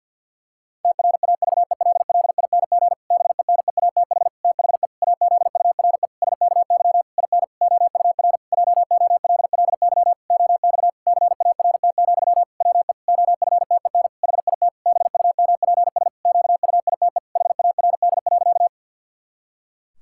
Morse code, heard as a keyed tone, about 48 wpm, 700 Hz. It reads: TRAVELLING BENEATH THE AZURE SKY IN OUR JOLLY OX CART, WE OFTEN HIT BUMPS QUITE HARD.
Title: Pangram Challenge v2 - 50wpm